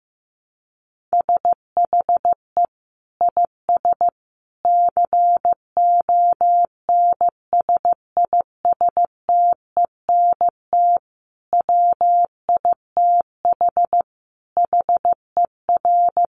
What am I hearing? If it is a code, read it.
Morse code, heard as a keyed tone, about 15 wpm, 700 Hz. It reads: SHE IS CONSISTENT WITH HER
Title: Sentences from Top 1500 Words 15wpm